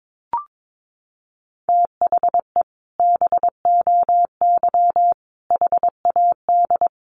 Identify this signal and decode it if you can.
Morse code, heard as a keyed tone, about 22 wpm, 700 Hz. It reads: THE BOY HAD
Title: Sentences from Top 1500 Words 22wpm